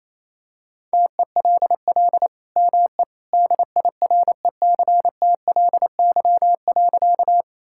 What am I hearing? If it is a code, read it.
Morse code, heard as a keyed tone, about 28 wpm, 700 Hz. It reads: TELL ME DIRECTLY.
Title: Sentences from Top 1000 Words - Review Part 5 - 28wpm